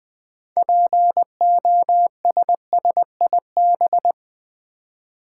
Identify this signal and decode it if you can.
Morse code, heard as a keyed tone, about 20 wpm, 700 Hz. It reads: POSSIB
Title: Build A Word - 2 Letters At A Time - 8-Letter Words - 20wpm